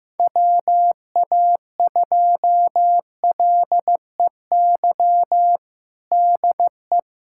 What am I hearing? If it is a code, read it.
Morse code, heard as a keyed tone, about 15 wpm, 700 Hz. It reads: WA2LEY DE
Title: USA Callsign DE Callsign 15wpm